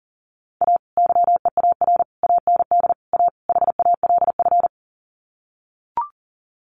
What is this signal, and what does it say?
Morse code, heard as a keyed tone, about 40 wpm, 700 Hz. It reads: A YEAR AND A HALF
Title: Sets of 5 Words 40wpm